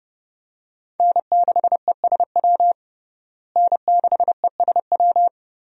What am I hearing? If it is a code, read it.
Morse code, heard as a keyed tone, about 32 wpm, 700 Hz. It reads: N6ESW N6ESW
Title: USA Call Signs - 5 Characters - Repeated 2x - 30wpm